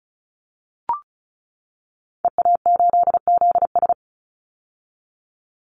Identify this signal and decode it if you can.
Morse code, heard as a keyed tone, about 36 wpm, 700 Hz. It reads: EA8ZS
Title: Call Signs - 35wpm